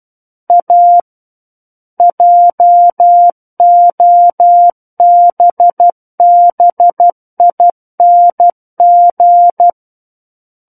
Morse code, heard as a keyed tone, about 12 wpm, 700 Hz. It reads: A JOBBING
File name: Book One / 03-Chapter 3 - 12 WPM